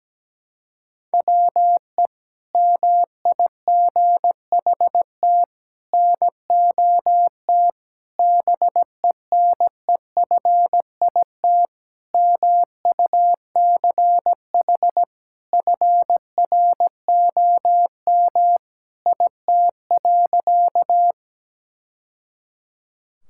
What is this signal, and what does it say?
Morse code, heard as a keyed tone, about 17 wpm, 700 Hz. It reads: WE MIGHT NOT BENEFIT MUCH FROM IT.
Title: Sentences from Top 1000 Words - Review Part 6 - 17wpm